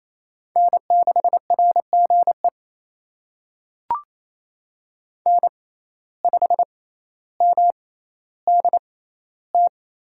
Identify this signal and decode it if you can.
Morse code, heard as a keyed tone, about 28 wpm, spaced as for 14 wpm, 700 Hz. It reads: N6RGE N5MDT
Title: USA Call Signs - 6x Character Spacing - 28wpm